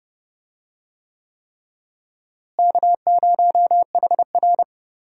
Morse code, heard as a keyed tone, about 30 wpm, 700 Hz. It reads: K0HR
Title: USA Call Signs - 4 Characters - Repeated 3x - 30wpm